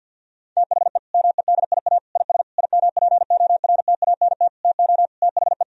Morse code, heard as a keyed tone, about 50 wpm, 700 Hz. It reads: THE MEDIA IS IMPORTANT TO THE
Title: Sentences from Top 1000 Words - Review Part 4 - 50wpm